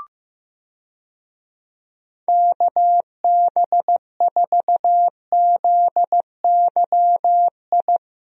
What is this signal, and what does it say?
Morse code, heard as a keyed tone, about 16 wpm, 700 Hz. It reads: KB4ZYI
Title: USA Call Signs - 6 Characters - Repeated 2x - 15wpm